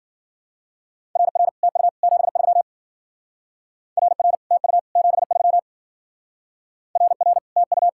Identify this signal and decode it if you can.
Morse code, heard as a keyed tone, about 48 wpm, 700 Hz. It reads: RR TU 73 RR TU 73 RR TU
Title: Speed-Racing Common QSO Phrases - 40wpm